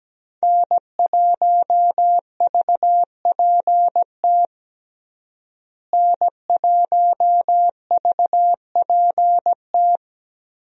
Morse code, heard as a keyed tone, about 17 wpm, 700 Hz. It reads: N1VPT N1VPT
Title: USA Call Signs - 5 Characters - Repeated 3x - 17wpm